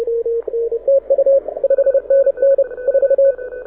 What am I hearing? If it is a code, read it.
Morse code, heard as a keyed tone, about 31 wpm, 550 Hz, with radio noise. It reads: TU 5NN 4
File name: audio zawody / IARU 2015 CW SN7Q